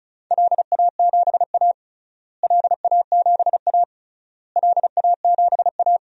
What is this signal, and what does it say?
Morse code, heard as a keyed tone, about 34 wpm, 700 Hz. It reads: LA7A LA7A LA7A
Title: DXCC-100 Call Signs - 4 Characters - Repeated 3x - 35wpm